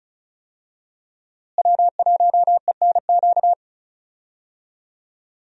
Morse code, heard as a keyed tone, about 36 wpm, 700 Hz. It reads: W1ENQ
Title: USA Call Signs - 5 Characters - Repeated 3x - 35wpm